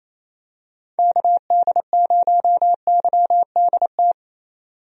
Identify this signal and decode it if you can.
Morse code, heard as a keyed tone, about 28 wpm, 700 Hz. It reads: KD0YDT
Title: USA Call Signs - 6 Characters - 28wpm